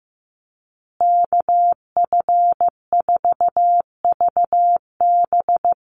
Morse code, heard as a keyed tone, about 15 wpm, 700 Hz. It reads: KF4VB
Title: USA Call Signs Repeated 2x 15wpm